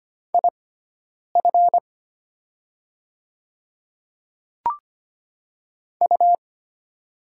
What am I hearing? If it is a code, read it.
Morse code, heard as a keyed tone, about 25 wpm, spaced as for 9 wpm, 700 Hz. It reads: IF U
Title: 6x Character Spacing - Top 100 Words - 25wpm